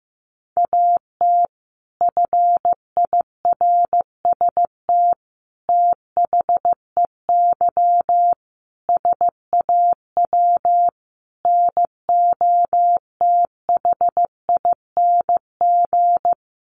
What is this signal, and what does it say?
Morse code, heard as a keyed tone, about 15 wpm, 700 Hz. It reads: AT FIRST THEY SAW NOTHING
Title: Sentences from Top 600 Words 15wpm